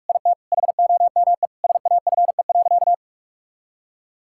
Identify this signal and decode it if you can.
Morse code, heard as a keyed tone, about 46 wpm, 700 Hz. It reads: IT HOME SAFE.
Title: Sentences from Top 1600 Words 45wpm